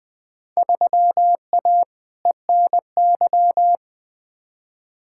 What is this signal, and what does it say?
Morse code, heard as a keyed tone, about 20 wpm, 700 Hz. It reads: 3A ENY
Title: Field Day Exchanges - 20wpm